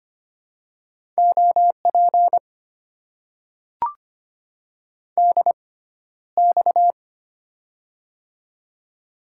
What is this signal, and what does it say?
Morse code, heard as a keyed tone, about 25 wpm, spaced as for 13 wpm, 700 Hz. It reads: OP DX
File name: Top 90 QSO Elements - 6x Character Spacing - 25wpm